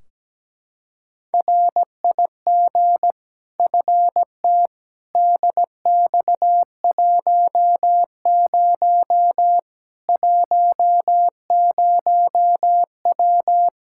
Morse code, heard as a keyed tone, about 17 wpm, 700 Hz. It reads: RIG FT DX10 10W
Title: Practice QSO - Exchange 2 - Info - 17wpm